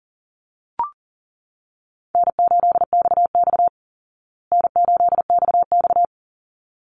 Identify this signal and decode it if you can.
Morse code, heard as a keyed tone, about 40 wpm, 700 Hz. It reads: N8XX N8XX
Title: Easy Call Signs Repeated 2x - 40wpm